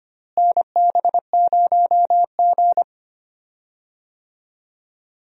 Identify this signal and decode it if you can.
Morse code, heard as a keyed tone, about 25 wpm, 700 Hz. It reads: NB0G
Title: USA Call Signs - 4 Characters - Repeated 3x - 25wpm